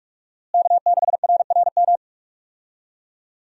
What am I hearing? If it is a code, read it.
Morse code, heard as a keyed tone, about 46 wpm, 700 Hz. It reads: K6RRM
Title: USA Call Signs - 5 Characters - Repeated 2x - 45wpm